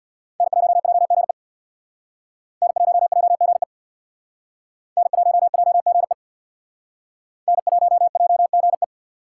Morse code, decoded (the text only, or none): N1JGE N1JGE N1JGE N1JGE